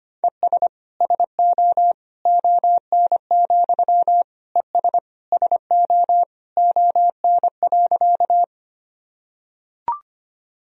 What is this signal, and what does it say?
Morse code, heard as a keyed tone, about 25 wpm, 700 Hz. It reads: ES SO ON, ES SO ON.
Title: Sentences from Top 100 Words and Abbreviations 25wpm